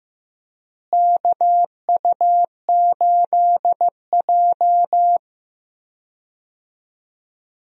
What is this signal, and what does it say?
Morse code, heard as a keyed tone, about 15 wpm, 700 Hz. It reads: KU8J
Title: USA Call Signs - 4 Characters - Repeated 2x - 15wpm